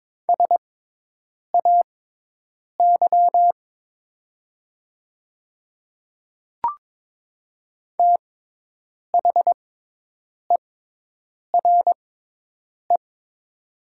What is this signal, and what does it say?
Morse code, heard as a keyed tone, about 22 wpm, spaced as for 8 wpm, 700 Hz. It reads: SAY THERE
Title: 6x Character Spacing - Top 300 Words - 22wpm